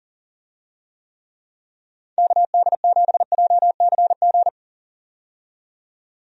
Morse code, heard as a keyed tone, about 38 wpm, 700 Hz. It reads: KD7JCG
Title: USA Call Signs - 6 Characters - 40wpm